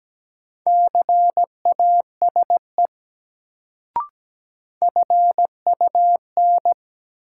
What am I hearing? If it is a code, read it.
Morse code, heard as a keyed tone, about 17 wpm, 700 Hz. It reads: CASE FUN
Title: Sentences from Top 700 Words - New Word Drill - 17wpm